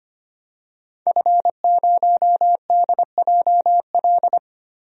Morse code, heard as a keyed tone, about 25 wpm, 700 Hz. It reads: F0DJL
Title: DXCC-100 Call Signs - 5 Characters - 25wpm